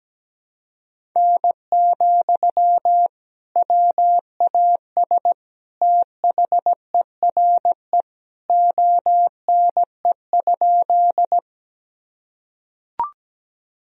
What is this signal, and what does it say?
Morse code, heard as a keyed tone, about 17 wpm, 700 Hz. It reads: N, WAS THERE ONE?
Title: Sentences from Top 100 Words and Abbreviations 17wpm